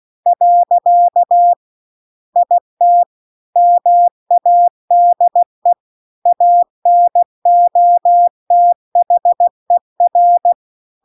Morse code, heard as a keyed tone, about 16 wpm, 700 Hz. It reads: . IT MADE ANOTHER
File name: CW-The-Big-Twoheaded-River-2-16WPM